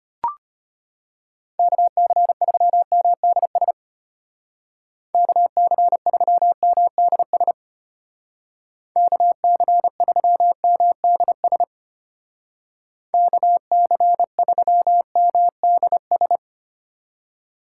Morse code, heard as a keyed tone, about 30 wpm, 700 Hz. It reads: KC3MDS KC3MDS KC3MDS KC3MDS
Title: Speed Racing - USA 2x3 Call Signs 25wpm